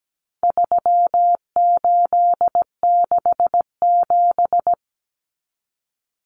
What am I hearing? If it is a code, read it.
Morse code, heard as a keyed tone, about 17 wpm, 700 Hz. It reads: 3867
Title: Numbers - 4-Digit 17wpm